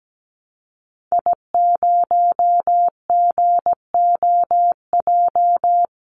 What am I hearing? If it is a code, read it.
Morse code, heard as a keyed tone, about 17 wpm, 700 Hz. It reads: I0GOJ
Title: Call Signs Repeated 2x - 17wpm